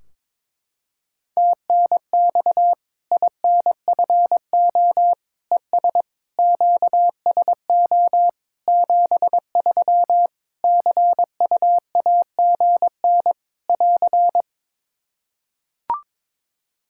Morse code, heard as a keyed tone, about 22 wpm, 700 Hz. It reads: TNX INFO ES QSO 73 CUAGN <AR>
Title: Practice QSO - Exchange 4 - Ending - 22wpm